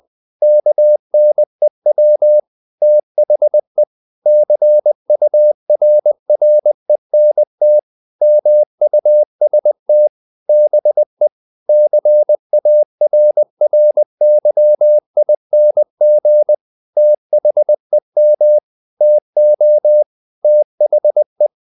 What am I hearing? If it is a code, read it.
Morse code, heard as a keyed tone, about 20 wpm, 600 Hz. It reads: KNEW THE CURRENT MUST BE CARRYING THEM TO THE
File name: the old man and the sea / the old man and the sea part 1 20wpm